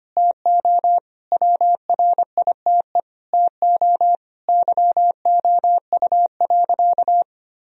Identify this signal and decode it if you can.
Morse code, heard as a keyed tone, about 25 wpm, 700 Hz. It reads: TO WRITE TO YOU.
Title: Sentences from Top 1000 Words - Review Part 1 - 25wpm